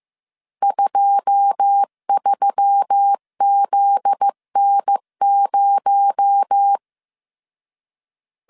Morse code, heard as a keyed tone, about 15 wpm, 800 Hz. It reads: 23ZN0
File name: CW / DET / 1 9 1500C S A 14wpm 3222